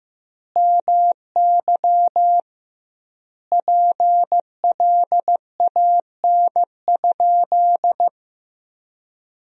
Morse code, heard as a keyed tone, about 15 wpm, 700 Hz. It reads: MY PLAN?
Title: 2x Word-Spacing - Sentences from Top 500 Words - 15wpm